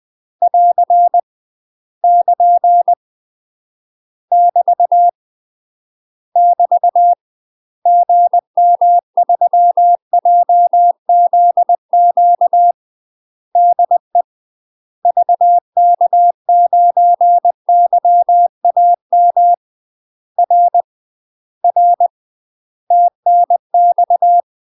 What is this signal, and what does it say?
Morse code, heard as a keyed tone, about 20 wpm, 700 Hz. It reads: <AR> <KN> <BT> <BT> GM3JZQ DE VK9YAM R R TNX
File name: UK MASTER QSOs TRAINING / 20WPM / UK MASTER QSOs 0039